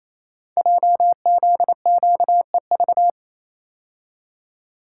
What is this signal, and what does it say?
Morse code, heard as a keyed tone, about 29 wpm, 700 Hz. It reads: JZQEV
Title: Code Groups - 5 Letters - 28wpm